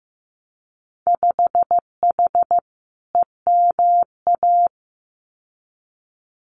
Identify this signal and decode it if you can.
Morse code, heard as a keyed tone, about 15 wpm, 700 Hz. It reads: 5H EMA
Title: Winter Field Day Exchanges Encore 15wpm